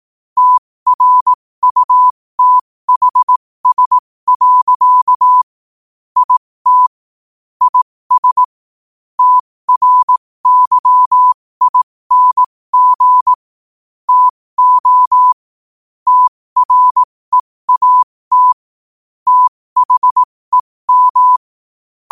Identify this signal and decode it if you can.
Morse code, heard as a keyed tone, about 18 wpm, spaced as for 15 wpm, 1000 Hz. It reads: TRUTHS. IT IS TRYING TO TREAT THEM
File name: qotdfiles / 15wpm